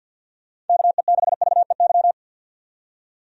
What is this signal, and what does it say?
Morse code, heard as a keyed tone, about 51 wpm, 700 Hz. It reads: KE6VEY
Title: USA Call Signs - 6 Characters - 50wpm